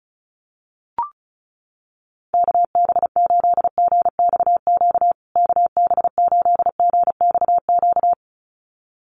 Morse code, heard as a keyed tone, about 34 wpm, 700 Hz. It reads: KB8GXQ KB8GXQ
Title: USA Call Signs Repeated 2x 35wpm